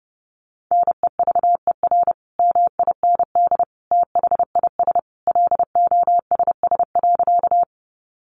Morse code, heard as a keyed tone, about 30 wpm, 700 Hz. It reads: NEVER MIND THIS LOSS.
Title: Sentences from Top 1400 Words 30wpm